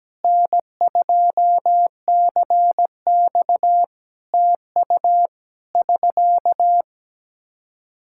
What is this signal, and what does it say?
Morse code, heard as a keyed tone, about 17 wpm, 700 Hz. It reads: N2CX TU <SK>
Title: Practice QSO - Exchange 4 - Ending - 17wpm